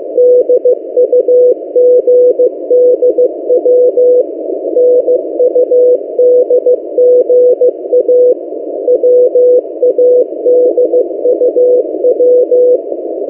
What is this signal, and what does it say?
Morse code, heard as a keyed tone, about 15 wpm, 500 Hz. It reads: DUGDW NUDGA WADUW
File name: M08a- -1355utc-8096khz